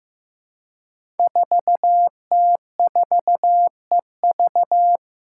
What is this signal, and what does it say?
Morse code, heard as a keyed tone, about 15 wpm, 700 Hz. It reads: 4T4EV
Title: Five Challenging Letter-Number Combinations 15wpm